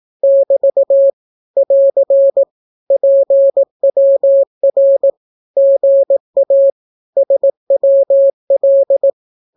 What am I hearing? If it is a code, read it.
Morse code, heard as a keyed tone, about 18 wpm, 550 Hz. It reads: <BT> <AR> PWR GA SWL
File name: naukapodstaw / skroty / skroty